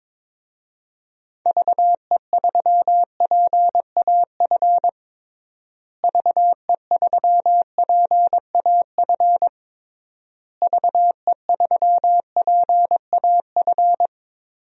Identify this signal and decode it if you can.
Morse code, heard as a keyed tone, about 22 wpm, 700 Hz. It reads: VE3PAF VE3PAF VE3PAF
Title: DXCC-100 Call Signs - 6 Characters - Repeated 3x - 22wpm